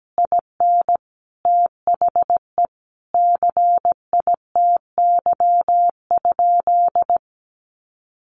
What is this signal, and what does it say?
Morse code, heard as a keyed tone, about 17 wpm, 700 Hz. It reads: IN THE CITY?
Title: Sentences from Top 1300 Words 17wpm